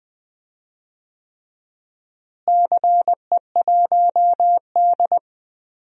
Morse code, heard as a keyed tone, about 20 wpm, 700 Hz. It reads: CE1D
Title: DXCC-100 Call Signs - 4 Characters - Repeated 3x - 20wpm